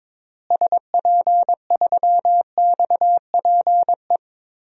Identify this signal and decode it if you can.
Morse code, heard as a keyed tone, about 23 wpm, 700 Hz. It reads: SP3XPE
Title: DXCC-100 Call Signs - 6 Characters - Repeated 2x - 22wpm